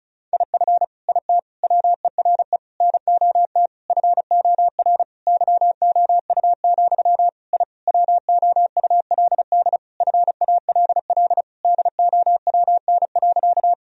Morse code, heard as a keyed tone, about 36 wpm, 700 Hz. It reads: IF IT WERE NOT FOR YOU, I WOULD FALL DOWN.
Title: Sentences from Top 1000 Words - Review Part 2 - 35wpm